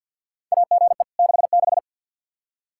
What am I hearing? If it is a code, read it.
Morse code, heard as a keyed tone, about 49 wpm, 700 Hz. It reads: AGE 66
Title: Practice QSO - Exchange 3 - More Info - 50wpm